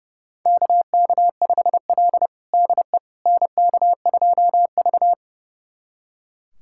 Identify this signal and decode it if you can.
Morse code, heard as a keyed tone, about 32 wpm, 700 Hz. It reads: KK5L DE NK2V
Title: USA Callsign DE Callsign - 4 Characters - 30wpm